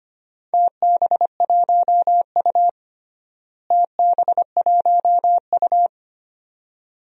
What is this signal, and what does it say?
Morse code, heard as a keyed tone, about 25 wpm, 700 Hz. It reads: TB1U TB1U
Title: DXCC-100 Call Signs - 4 Characters - Repeated 2x - 25wpm